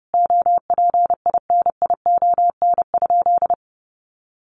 Morse code, heard as a keyed tone, about 28 wpm, 700 Hz. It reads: OPINION?
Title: Sentences from Top 1200 Words 30wpm